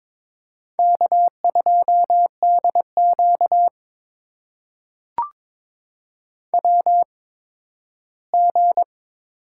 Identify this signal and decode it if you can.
Morse code, heard as a keyed tone, about 22 wpm, spaced as for 13 wpm, 700 Hz. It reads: K2DQ WG
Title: USA Call Signs - 8x Character Spacing - 22wpm